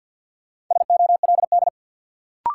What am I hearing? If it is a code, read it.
Morse code, heard as a keyed tone, about 53 wpm, 700 Hz. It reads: SOLD
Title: Sentences from Top 1400 Words - New Word Drill - 50wpm